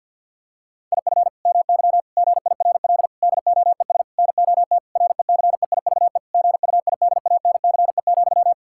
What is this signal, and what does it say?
Morse code, heard as a keyed tone, about 50 wpm, 700 Hz. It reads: IF MY GIRL DOES NOT RECEIVE GUIDANCE,
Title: Sentences from Top 1600 Words 50wpm